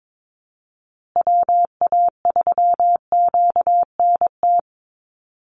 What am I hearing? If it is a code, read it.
Morse code, heard as a keyed tone, about 22 wpm, 700 Hz. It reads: WA3QNT
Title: Call Signs Repeated 2x - 22wpm